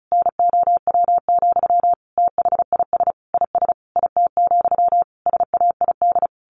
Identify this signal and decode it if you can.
Morse code, heard as a keyed tone, about 34 wpm, 700 Hz. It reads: NOW, THIS IS IT, SAID
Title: A Sound Of Thunder 35wpm